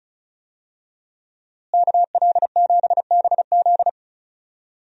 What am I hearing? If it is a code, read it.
Morse code, heard as a keyed tone, about 36 wpm, 700 Hz. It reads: KL7BZ
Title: DXCC-100 Call Signs - 5 Characters - Repeated 2x - 35wpm